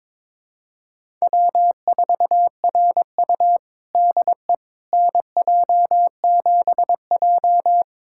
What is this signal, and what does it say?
Morse code, heard as a keyed tone, about 22 wpm, 700 Hz. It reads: W4RU DE NJ7J
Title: USA Callsign DE Callsign - 4 Characters - 22wpm